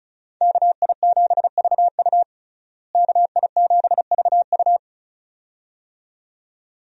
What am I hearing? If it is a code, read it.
Morse code, heard as a keyed tone, about 36 wpm, 700 Hz. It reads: KI7VU KI7VU
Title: USA Call Signs - 5 Characters - Repeated 2x - 35wpm